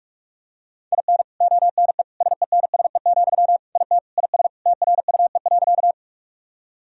Morse code, heard as a keyed tone, about 41 wpm, 700 Hz. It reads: IN ONE SENSE, IT IS TRUE.
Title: Sentences from Top 1000 Words - Review Part 4 - 45wpm